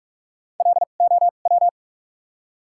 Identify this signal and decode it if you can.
Morse code, heard as a keyed tone, about 46 wpm, spaced as for 33 wpm, 700 Hz. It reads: LOW
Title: 2x Character Spacing - Top 300 Words - 45wpm